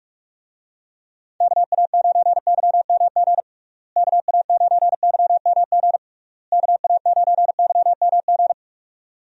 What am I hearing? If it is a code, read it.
Morse code, heard as a keyed tone, about 46 wpm, 700 Hz. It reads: KA9YMG KA9YMG KA9YMG
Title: USA Call Signs - 6 Characters - Repeated 3x - 45wpm